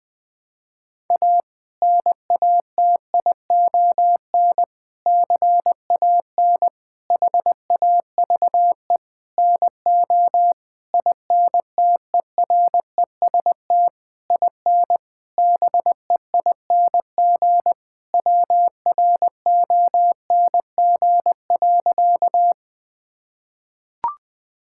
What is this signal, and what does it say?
Morse code, heard as a keyed tone, about 20 wpm, 700 Hz. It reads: A NATION CAN HAVE NO INTEREST IN BEING WRONG.
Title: Sentences from Top 1000 Words - Review Part 6 - 20wpm